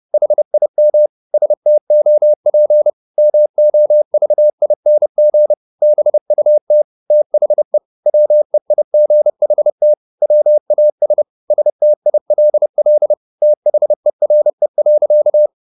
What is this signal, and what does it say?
Morse code, heard as a keyed tone, about 30 wpm, 600 Hz. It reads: HIM STOP MOVING BUT THE WEIGHT WAS STILL THERE.
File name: the old man and the sea / the old man and the sea 1 30wpm